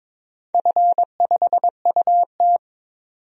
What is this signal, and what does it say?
Morse code, heard as a keyed tone, about 22 wpm, 700 Hz. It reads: F5UT
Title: DXCC-100 Call Signs - 4 Characters - 22wpm